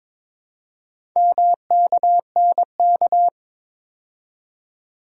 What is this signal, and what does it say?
Morse code, heard as a keyed tone, about 22 wpm, 700 Hz. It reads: MKNK
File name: practice / 22wpm / Four Letter-Number Combinations - 22wpm